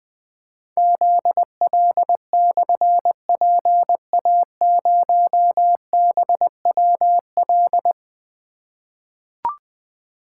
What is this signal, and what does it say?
Morse code, heard as a keyed tone, about 20 wpm, 700 Hz. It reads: ZL/PA0BWL
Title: Difficult Call Signs 20wpm